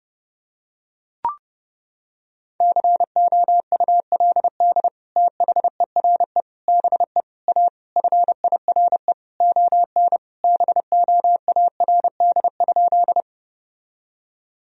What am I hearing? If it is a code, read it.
Morse code, heard as a keyed tone, about 30 wpm, 700 Hz. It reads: COULD THERE BE A FIRE ON BOARD?
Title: Sentences from Top 1000 Words - Review Part 4 - 30wpm